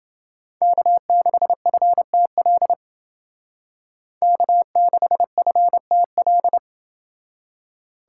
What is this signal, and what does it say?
Morse code, heard as a keyed tone, about 29 wpm, 700 Hz. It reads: K6FTL K6FTL